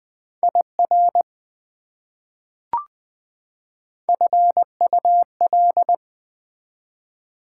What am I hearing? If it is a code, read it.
Morse code, heard as a keyed tone, about 20 wpm, 700 Hz. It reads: IR FUL
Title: Prefixes and Suffixes 20wpm